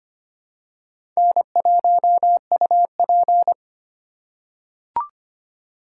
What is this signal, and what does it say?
Morse code, heard as a keyed tone, about 25 wpm, 700 Hz. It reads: N1UP
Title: USA Call Signs - 3x Character Spacing - 25wpm